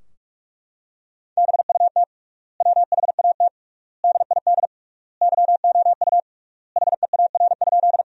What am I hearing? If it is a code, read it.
Morse code, heard as a keyed tone, about 43 wpm, 700 Hz. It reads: BUT WHAT DID YOU HEAR?
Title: 3x Word-Spacing - Sentences from Top 300 Words - 45wpm